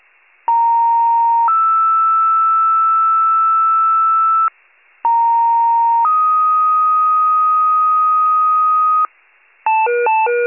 Two-tone page, heard as a one-second tone, then a three-second tone, about 930 then 1320 Hz, about 930 then 1250 Hz. It is heard from 0.5 to 9.1 s.